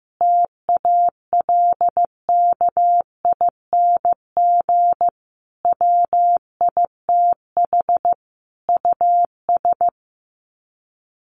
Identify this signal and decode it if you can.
Morse code, heard as a keyed tone, about 15 wpm, 700 Hz. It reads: TALKING WITH US
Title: Sets of 5 Words Encore 15wpm